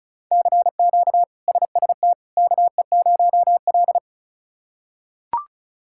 Morse code, heard as a keyed tone, about 34 wpm, 700 Hz. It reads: CQ SST KE0L
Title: K1USN SST QSO Practice - 35wpm